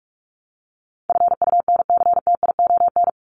Morse code, heard as a keyed tone, about 44 wpm, 700 Hz. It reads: FUNCTION
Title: ES:S Suffix - Top 100 Words with Base Words 45wpm